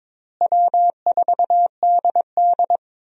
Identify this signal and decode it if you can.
Morse code, heard as a keyed tone, about 22 wpm, 700 Hz. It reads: W4DD
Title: USA Call Signs - 4 Characters - Repeated 3x - 22wpm